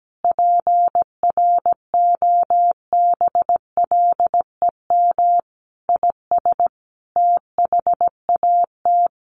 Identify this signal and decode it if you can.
Morse code, heard as a keyed tone, about 17 wpm, 700 Hz. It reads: PROBLEM IS THAT
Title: Sets of 5 Words Encore2 17wpm